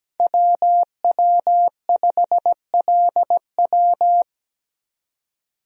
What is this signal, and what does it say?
Morse code, heard as a keyed tone, about 17 wpm, 700 Hz. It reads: WW5LW
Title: USA Call Signs - 5 Characters - 17wpm